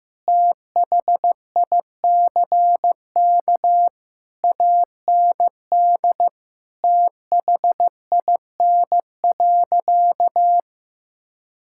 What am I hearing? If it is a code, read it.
Morse code, heard as a keyed tone, about 15 wpm, 700 Hz. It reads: THICK AND THIN.
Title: Sentences from Top 1600 Words 15wpm